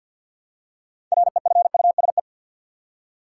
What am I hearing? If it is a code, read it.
Morse code, heard as a keyed tone, about 47 wpm, 700 Hz. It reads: REFUSE
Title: Six Letter Words - 50wpm